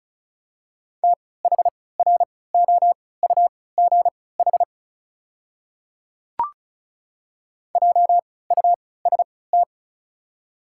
Morse code, heard as a keyed tone, about 36 wpm, spaced as for 20 wpm, 690 Hz. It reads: THROUGH JUST
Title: 3x Character Spacing - Top 300 Words - 35wpm